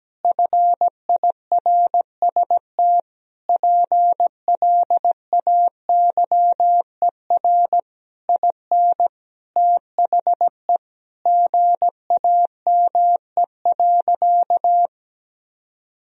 Morse code, heard as a keyed tone, about 17 wpm, 700 Hz. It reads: FIRST PLAYER IN THE GAME.
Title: Sentences from Top 1000 Words - Review Part 4 - 17wpm